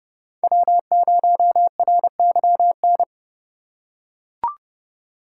Morse code, heard as a keyed tone, about 30 wpm, 700 Hz. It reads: W0RYN
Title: USA Call Signs - 5 Characters - 2x Character Spacing - 30wpm